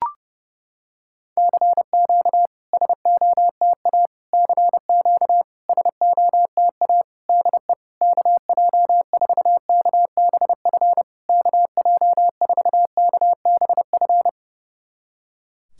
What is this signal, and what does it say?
Morse code, heard as a keyed tone, about 30 wpm, 700 Hz. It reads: CQ SOTA CQ SOTA DE KJ4KBF KJ4KBF
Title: SOTA QSO Practice 30wpm